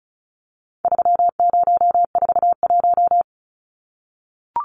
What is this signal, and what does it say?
Morse code, heard as a keyed tone, about 34 wpm, 700 Hz. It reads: 3041
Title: Numbers - 4-Digit 35wpm